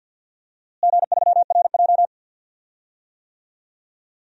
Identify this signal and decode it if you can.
Morse code, heard as a keyed tone, about 46 wpm, 700 Hz. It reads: G3RJ
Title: DXCC-100 Call Signs - 4 Characters - Repeated 2x - 50wpm